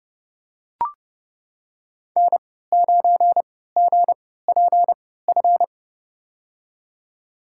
Morse code, heard as a keyed tone, about 30 wpm, spaced as for 17 wpm, 700 Hz. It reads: N9GPF
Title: USA Call Signs - 5 Characters - 3x Character Spacing - 30wpm